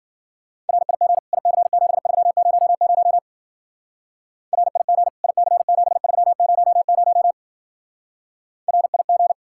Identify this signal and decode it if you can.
Morse code, heard as a keyed tone, about 47 wpm, 700 Hz. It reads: RIG IC7300 RIG IC7300 RIG
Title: Speed-Racing Common QSO Phrases - 40wpm